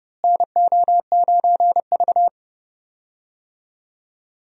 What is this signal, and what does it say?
Morse code, heard as a keyed tone, about 29 wpm, 700 Hz. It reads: NO9V
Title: USA Call Signs - 4 Characters - Repeated 2x - 30wpm